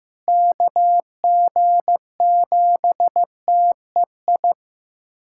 Morse code, heard as a keyed tone, about 15 wpm, 700 Hz. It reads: KG7TEI
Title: USA Callsign DE Callsign - 6 Characters - 15wpm